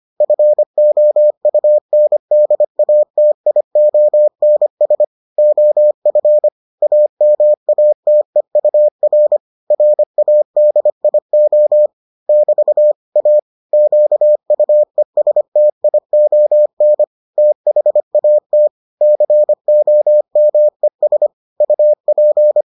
Morse code, heard as a keyed tone, about 25 wpm, 600 Hz. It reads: FOUNDATIONS OF AMATEUR RADIO <BT> A QUESTION THAT COMES UP